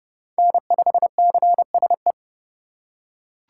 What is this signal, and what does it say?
Morse code, heard as a keyed tone, about 30 wpm, 700 Hz. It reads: N5CSE
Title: USA Call Signs - 5 Characters - 4x Character Spacing - 30wpm